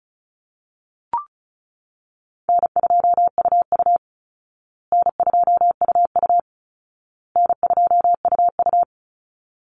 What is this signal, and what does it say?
Morse code, heard as a keyed tone, about 34 wpm, 700 Hz. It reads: N2UU N2UU N2UU
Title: Call Signs Repeated 2x - 35wpm